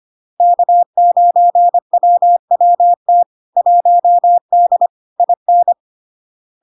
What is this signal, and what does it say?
Morse code, heard as a keyed tone, about 25 wpm, 700 Hz. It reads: K9WWT 1D IN
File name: LARK-FD-2022-Log-in-CW-25wpm-700-Hz